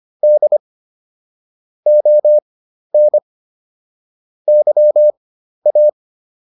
Morse code, heard as a keyed tone, about 25 wpm, spaced as for 12 wpm, 600 Hz. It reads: D ON YA